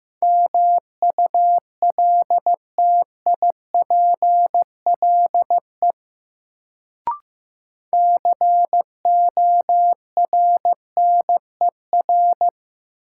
Morse code, heard as a keyed tone, about 15 wpm, 700 Hz. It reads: MULTIPLE CORNER
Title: Sets of 5 Words - New Word Drill - 15wpm